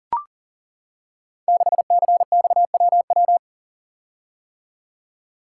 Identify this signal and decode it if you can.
Morse code, heard as a keyed tone, about 42 wpm, 700 Hz. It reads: 6CXWW
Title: Code Groups - 5 Letter:Numbers - 40wpm